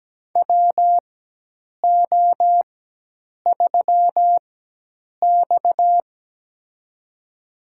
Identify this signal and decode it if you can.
Morse code, heard as a keyed tone, about 17 wpm, spaced as for 8 wpm, 700 Hz. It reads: WO3X
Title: USA Call Signs - 4 Characters - 4x Character Spacing - 17wpm